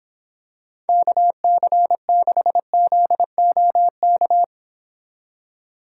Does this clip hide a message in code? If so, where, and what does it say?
Yes: Morse code from 0.9 to 4.4 s, heard as a keyed tone, about 26 wpm, 700 Hz. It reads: KC6ZOK